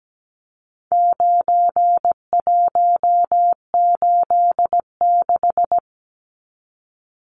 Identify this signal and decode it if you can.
Morse code, heard as a keyed tone, about 17 wpm, 700 Hz. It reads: 9186
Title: Numbers - 4-Digit 17wpm